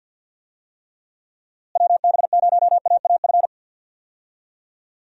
Morse code, heard as a keyed tone, about 51 wpm, 700 Hz. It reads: WB0AAF
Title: USA Call Signs - 6 Characters - Repeated 3x - 50wpm